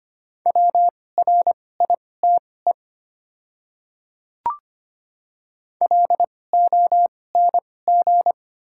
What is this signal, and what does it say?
Morse code, heard as a keyed tone, about 25 wpm, spaced as for 18 wpm, 700 Hz. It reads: WRITE LONG
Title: 2x Character Spacing - Top 100 Words - 25wpm